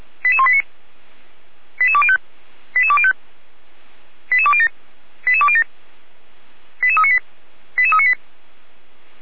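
Five-tone selective call (ZVEI), heard as quick runs of five tones, tones of about 70 ms. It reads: 88189 88205 88205 88207 88207 88398 88398
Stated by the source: ZVEI five-tone selective call received by SDR.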